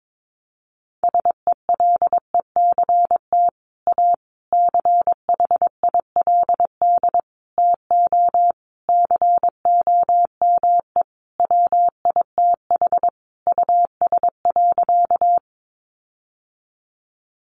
Morse code, heard as a keyed tone, about 22 wpm, 700 Hz. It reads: SELECT A CHILD TO COME WITH US.
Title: Sentences from Top 1300 Words 22wpm